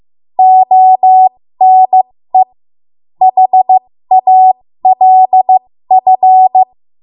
Morse code, heard as a keyed tone, about 15 wpm, spaced as for 13 wpm, 750 Hz. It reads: ONE HALF
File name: Morse / 13WPM